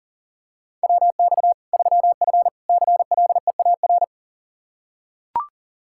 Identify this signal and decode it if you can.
Morse code, heard as a keyed tone, about 40 wpm, 700 Hz. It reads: WX 3F CLEAR
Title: K1USN SST QSO Practice 40wpm